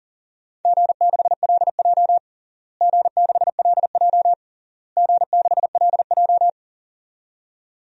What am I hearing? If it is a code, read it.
Morse code, heard as a keyed tone, about 37 wpm, 700 Hz. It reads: G6LJ G6LJ G6LJ
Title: DXCC-100 Call Signs - 4 Characters - Repeated 3x - 40wpm